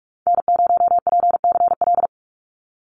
Morse code, heard as a keyed tone, about 46 wpm, 700 Hz. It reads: N0PCL
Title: NA SOTA Activator Callsigns 45wpm